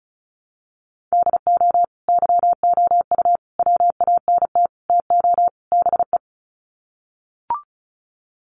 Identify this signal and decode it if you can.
Morse code, heard as a keyed tone, about 34 wpm, 700 Hz. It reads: DO YOU WANT TO BE
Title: Sets of 5 Words Encore2 35wpm